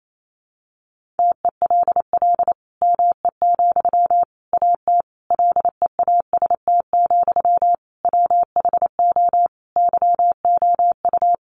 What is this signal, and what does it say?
Morse code, heard as a keyed tone, about 28 wpm, 700 Hz. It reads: TELL ME, AT LEAST, WHO YOU
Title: Sentences from Top 700 Words 28wpm